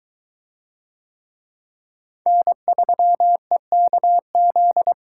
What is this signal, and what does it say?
Morse code, heard as a keyed tone, about 21 wpm, 700 Hz. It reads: N3EKZ